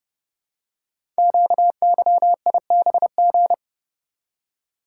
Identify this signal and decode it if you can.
Morse code, heard as a keyed tone, about 30 wpm, 700 Hz. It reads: QYIBG
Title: Code Groups - 5 Letters - 30wpm